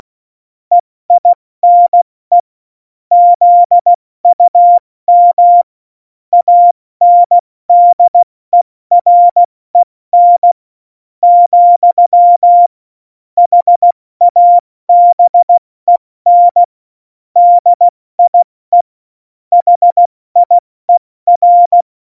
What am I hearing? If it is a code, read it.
Morse code, heard as a keyed tone, about 16 wpm, spaced as for 14 wpm, 700 Hz. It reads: EINE ZUM ANDEREN, HABEN DIE HIER
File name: Morsekurs / Witze / Witz 24.Juli 2018, 70 F80